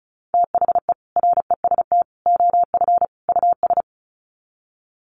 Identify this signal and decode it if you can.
Morse code, heard as a keyed tone, about 34 wpm, 700 Hz. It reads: THE REST OF US
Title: Sets of 4 Words 35wpm